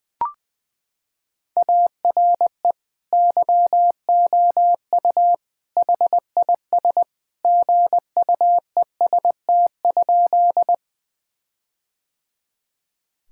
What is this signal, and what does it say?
Morse code, heard as a keyed tone, about 20 wpm, 700 Hz. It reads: ARE YOU HIS GUEST?
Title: Sentences from Top 1600 Words 20wpm